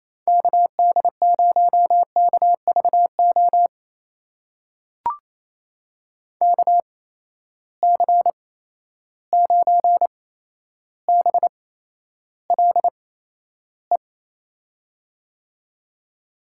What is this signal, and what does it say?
Morse code, heard as a keyed tone, about 28 wpm, spaced as for 12 wpm, 700 Hz. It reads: KD0KVO KC9BLE
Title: USA Call Signs - 8x Character Spacing - 28wpm